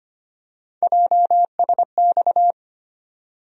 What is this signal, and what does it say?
Morse code, heard as a keyed tone, about 25 wpm, 700 Hz. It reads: JSX
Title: USA Call Sign Postfixes - 3 Letters 25wpm